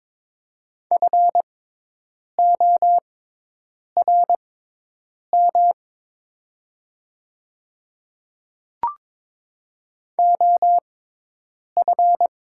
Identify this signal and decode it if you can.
Morse code, heard as a keyed tone, about 22 wpm, spaced as for 8 wpm, 700 Hz. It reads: FORM OF
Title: 6x Character Spacing - Top 300 Words - 22wpm